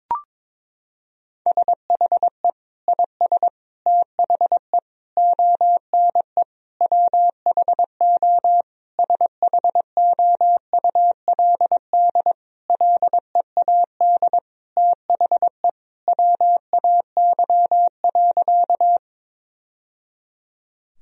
Morse code, heard as a keyed tone, about 22 wpm, 700 Hz. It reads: SHE IS THE ONE WHO SHOULD LEAD THE WAY.
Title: Sentences from Top 1000 Words - Review Part 2 - 22wpm